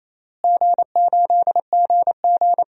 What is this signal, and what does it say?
Morse code, heard as a keyed tone, about 28 wpm, 700 Hz. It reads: G8GG
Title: DXCC-100 Call Signs - 4 Characters - Repeated 2x - 28wpm